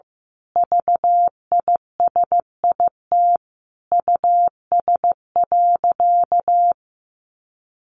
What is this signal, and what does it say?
Morse code, heard as a keyed tone, about 15 wpm, 700 Hz. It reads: VISIT US.
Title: Sentences from Top 1400 Words 15wpm